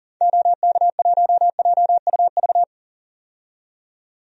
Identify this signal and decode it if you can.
Morse code, heard as a keyed tone, about 40 wpm, 700 Hz. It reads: OK1JUV
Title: DXCC-100 Call Signs - 6 Characters - 40wpm